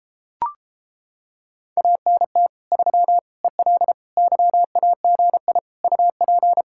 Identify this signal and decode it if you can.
Morse code, heard as a keyed tone, about 29 wpm, 700 Hz. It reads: ANT 3 EL YAGI UP